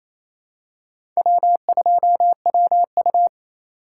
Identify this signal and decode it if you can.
Morse code, heard as a keyed tone, about 28 wpm, 700 Hz. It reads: W2WU
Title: USA Call Signs - 2x Character Spacing - 28wpm